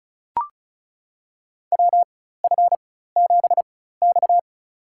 Morse code, heard as a keyed tone, about 36 wpm, spaced as for 16 wpm, 700 Hz. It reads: WF7X
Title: USA Call Signs - 4 Characters - 4x Character Spacing - 35wpm